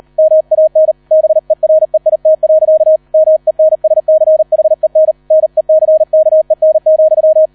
Morse code, heard as a keyed tone, about 38 wpm, 630 Hz. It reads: MAN BEREIT. MENSCHEN NECKEN,